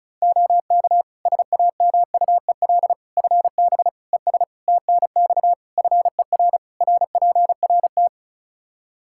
Morse code, heard as a keyed tone, about 34 wpm, 700 Hz. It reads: OK SAMUEL FB ES TNX FER RPRT
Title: K1USN SST QSO Practice 35wpm